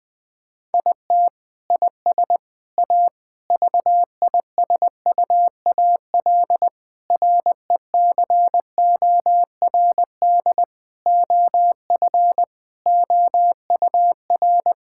Morse code, heard as a keyed tone, about 20 wpm, 700 Hz. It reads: IT IS A VISUAL RECORD OF OUR
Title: Sentences from Top 1000 Words - Review Part 6 - 20wpm